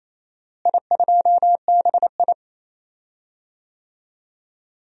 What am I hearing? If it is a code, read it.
Morse code, heard as a keyed tone, about 28 wpm, 700 Hz. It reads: I2BI
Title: DXCC-100 Call Signs - 4 Characters - Repeated 2x - 28wpm